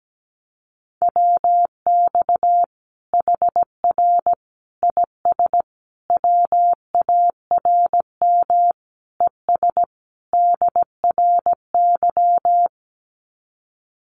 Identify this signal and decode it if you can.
Morse code, heard as a keyed tone, about 17 wpm, 700 Hz. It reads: WX HR IS WARM ES DRY
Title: Common QSO Phrases 17wpm